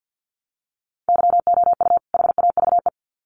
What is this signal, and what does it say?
Morse code, heard as a keyed tone, about 50 wpm, 700 Hz. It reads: YOU HAVE
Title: Sets of 2 Words 50wpm